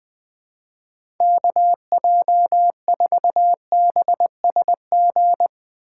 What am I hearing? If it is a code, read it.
Morse code, heard as a keyed tone, about 20 wpm, 700 Hz. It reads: KJ4BSG
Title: USA Call Signs - 6 Characters - 3x Character Spacing - 20wpm